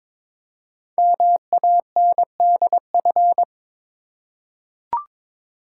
Morse code, heard as a keyed tone, about 22 wpm, 700 Hz. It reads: MANDF
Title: Code Groups - 5 Letters - 22wpm